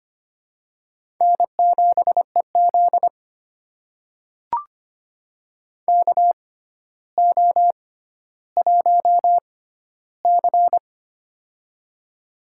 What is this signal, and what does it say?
Morse code, heard as a keyed tone, about 25 wpm, spaced as for 13 wpm, 700 Hz. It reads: N7EZ KO1C
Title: USA Call Signs - 4 Characters - 6x Character Spacing - 25wpm